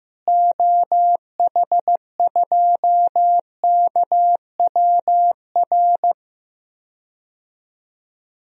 Morse code, heard as a keyed tone, about 15 wpm, 700 Hz. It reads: OH2KWR
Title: DXCC-100 Call Signs - 6 Characters - Repeated 2x - 15wpm